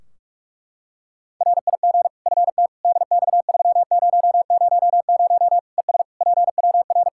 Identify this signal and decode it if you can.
Morse code, heard as a keyed tone, about 41 wpm, 700 Hz. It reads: RIG FT DX3000 ES PWR
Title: K1USN SST QSO Practice 45wpm